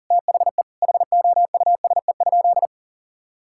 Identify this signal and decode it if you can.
Morse code, heard as a keyed tone, about 40 wpm, 700 Hz. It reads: THE HOUSE?
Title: Sentences from Top 1000 Words - Review Part 5 - 40wpm